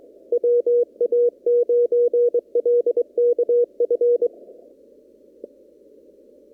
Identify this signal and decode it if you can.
Morse code, heard as a keyed tone, about 24 wpm, 480 Hz. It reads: WA9LKF
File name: CW / kt5j wpx cw-op-k5pi